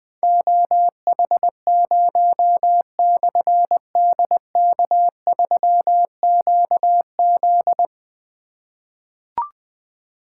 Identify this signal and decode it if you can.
Morse code, heard as a keyed tone, about 21 wpm, 700 Hz. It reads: OH0/DK3QZ
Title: Difficult Call Signs 20wpm